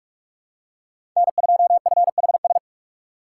Morse code, heard as a keyed tone, about 41 wpm, 700 Hz. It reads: N2FHS
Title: USA Call Signs - 5 Characters - 6x Character Spacing - 45wpm